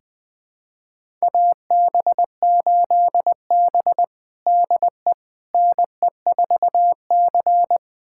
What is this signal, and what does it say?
Morse code, heard as a keyed tone, about 20 wpm, 700 Hz. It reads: AB8B DE NE4C
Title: USA Callsign DE Callsign - 4 Characters - 20wpm